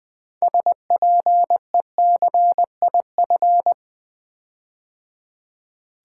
Morse code, heard as a keyed tone, about 20 wpm, 700 Hz. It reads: SPECIF
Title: Build A Word - 6 Letters At A Time - 12-Letter Words - 20wpm